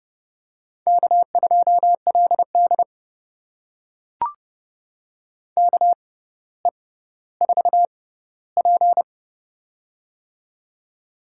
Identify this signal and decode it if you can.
Morse code, heard as a keyed tone, about 30 wpm, spaced as for 15 wpm, 700 Hz. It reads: K2LD KE4P
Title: USA Call Signs - 4 Characters - 6x Character Spacing - 30wpm